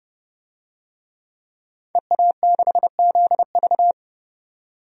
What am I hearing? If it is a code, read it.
Morse code, heard as a keyed tone, about 30 wpm, 700 Hz. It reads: EA6ZV
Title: DXCC-100 Call Signs - 5 Characters - 30wpm